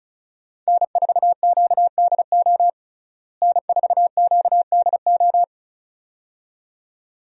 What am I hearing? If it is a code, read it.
Morse code, heard as a keyed tone, about 34 wpm, 700 Hz. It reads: N4QDO N4QDO
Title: USA Call Signs - 5 Characters - Repeated 2x - 35wpm